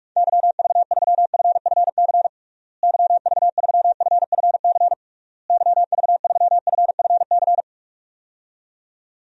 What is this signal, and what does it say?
Morse code, heard as a keyed tone, about 41 wpm, 700 Hz. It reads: YV3FFC YV3FFC YV3FFC
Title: DXCC-100 Call Signs - 6 Characters - Repeated 3x - 45wpm